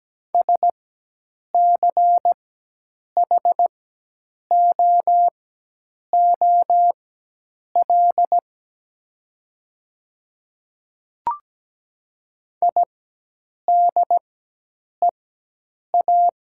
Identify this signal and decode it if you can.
Morse code, heard as a keyed tone, about 17 wpm, spaced as for 8 wpm, 700 Hz. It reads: SCHOOL IDEA
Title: 4x Character Spacing - Top 300 Words - 17wpm